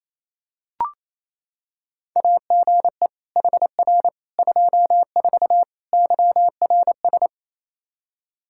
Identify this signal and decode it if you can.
Morse code, heard as a keyed tone, about 27 wpm, 700 Hz. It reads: AGE HR 24 YRS
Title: K1USN SST QSO Practice 28wpm